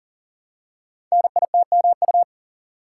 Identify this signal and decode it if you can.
Morse code, heard as a keyed tone, about 42 wpm, 690 Hz. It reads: NITMU
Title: Code Groups - 5 Letter:Numbers - 40wpm